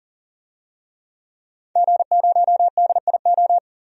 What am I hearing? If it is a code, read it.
Morse code, heard as a keyed tone, about 40 wpm, 700 Hz. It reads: G0DIO
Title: DXCC-100 Call Signs - 5 Characters - 40wpm